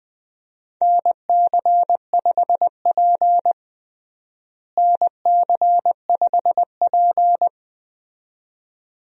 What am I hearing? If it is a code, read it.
Morse code, heard as a keyed tone, about 20 wpm, 700 Hz. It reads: NC5P NC5P
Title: USA Call Signs - 4 Characters - Repeated 2x - 20wpm